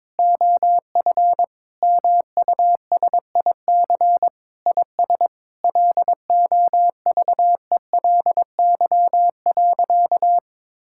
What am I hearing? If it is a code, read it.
Morse code, heard as a keyed tone, about 22 wpm, 700 Hz. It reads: OF MUSIC IS LOVELY.
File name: practice / 22wpm / Sentences with 3-Word Phrases - 22wpm